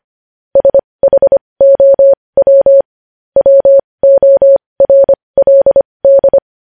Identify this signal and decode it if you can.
Morse code, heard as a keyed tone, about 25 wpm, spaced as for 20 wpm, 550 Hz. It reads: SHOW WORLD
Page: news headlines in morse code at 20 wpm